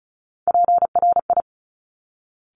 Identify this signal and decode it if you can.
Morse code, heard as a keyed tone, about 36 wpm, 690 Hz. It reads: PRI
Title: Top Three Letter Combinations in Words 35wpm